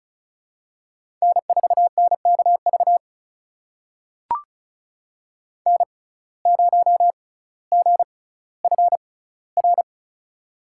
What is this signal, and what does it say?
Morse code, heard as a keyed tone, about 34 wpm, spaced as for 18 wpm, 700 Hz. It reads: N4NKV N0GFR
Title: USA Call Signs - 5 Characters - 6x Character Spacing - 35wpm